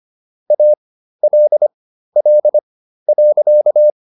Morse code, heard as a keyed tone, about 25 wpm, spaced as for 13 wpm, 600 Hz. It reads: ALL.